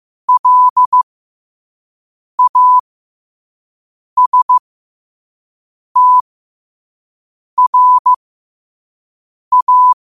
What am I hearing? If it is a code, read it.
Morse code, heard as a keyed tone, about 15 wpm, spaced as for 5 wpm, 1000 Hz. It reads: LASTRA
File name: qotdfiles / 5wpm